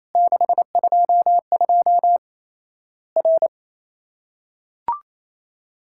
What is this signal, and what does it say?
Morse code, heard as a keyed tone, about 28 wpm, 700 Hz. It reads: 622 R
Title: Worked All Europe DX Contest - QTC Exchange - 28wpm